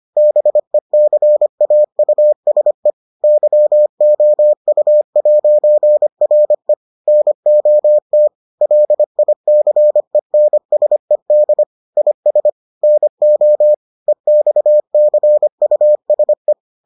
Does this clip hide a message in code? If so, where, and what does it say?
Yes: Morse code from 0.2 to 16.5 s, heard as a keyed tone, about 25 wpm, 600 Hz. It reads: BECAUSE YOU'RE NOT LICENSED IS NO EXCUSE